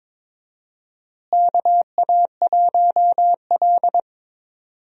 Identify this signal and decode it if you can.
Morse code, heard as a keyed tone, about 22 wpm, 700 Hz. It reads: KA1L
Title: USA Call Signs - 4 Characters - Repeated 2x - 22wpm